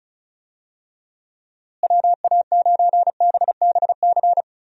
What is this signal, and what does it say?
Morse code, heard as a keyed tone, about 36 wpm, 700 Hz. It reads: WA9BBC
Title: USA Call Signs - 6 Characters - Repeated 2x - 35wpm